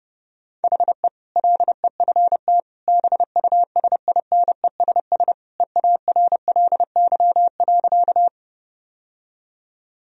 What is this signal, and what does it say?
Morse code, heard as a keyed tone, about 32 wpm, 700 Hz. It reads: HE LEFT BUSINESS EARLY.
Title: Sentences from Top 1000 Words - Review Part 2 - 30wpm